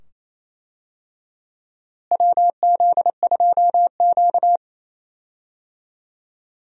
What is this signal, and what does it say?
Morse code, heard as a keyed tone, about 28 wpm, 700 Hz. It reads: WZ2Q
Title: USA Call Signs - 4 Characters - Repeated 2x - 28wpm